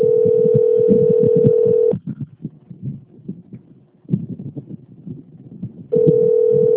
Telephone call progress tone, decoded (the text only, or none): ringback tone